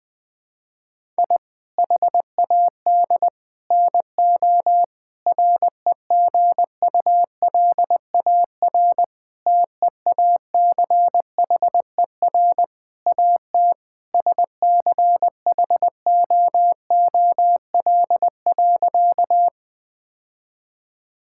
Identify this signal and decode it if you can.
Morse code, heard as a keyed tone, about 21 wpm, 700 Hz. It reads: I HAD NO REGULAR TEACHER AT SCHOOL.
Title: Sentences from Top 1000 Words - Review Part 6 - 20wpm